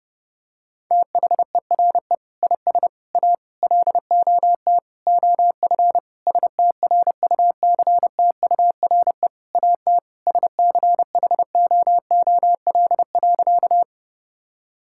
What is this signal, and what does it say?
Morse code, heard as a keyed tone, about 32 wpm, 700 Hz. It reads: THERE IS A LOT OF STRUCTURE AT SCHOOL.
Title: Sentences from Top 1000 Words - Review Part 6 - 30wpm